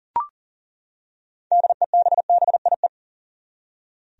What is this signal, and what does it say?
Morse code, heard as a keyed tone, about 38 wpm, 700 Hz. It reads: DEBBIE
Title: USA Common Names - 40wpm